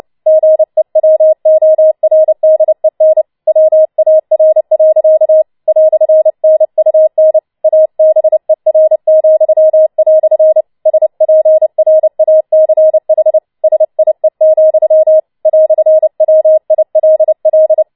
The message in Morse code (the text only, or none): GEWORDEN WAR. "NUN ABER," SPRACH SIE, "WILL